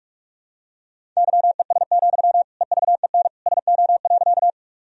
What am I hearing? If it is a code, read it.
Morse code, heard as a keyed tone, about 41 wpm, 700 Hz. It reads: YES, EVEN SO.
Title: Sentences from Top 1000 Words - Review Part 1 - 45wpm